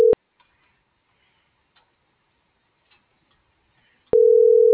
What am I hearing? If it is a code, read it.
Ringback tone (North America), 440 + 480 Hz.